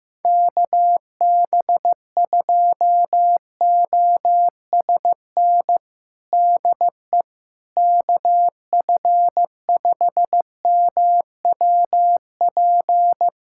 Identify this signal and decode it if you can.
Morse code, heard as a keyed tone, about 16 wpm, 700 Hz. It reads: KB2OSN DE KF5MWP
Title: USA Callsign DE Callsign 15wpm